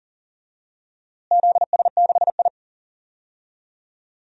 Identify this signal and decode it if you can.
Morse code, heard as a keyed tone, about 40 wpm, 700 Hz. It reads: ZS6I
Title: DXCC-100 Call Signs - 4 Characters - Repeated 3x - 40wpm